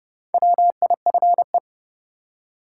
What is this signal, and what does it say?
Morse code, heard as a keyed tone, about 30 wpm, 700 Hz. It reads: WIFE
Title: Four Letter Words - 30wpm